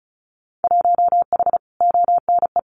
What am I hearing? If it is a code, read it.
Morse code, heard as a keyed tone, about 34 wpm, 700 Hz. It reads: 1H ONE
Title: Winter Field Day Exchanges 35wpm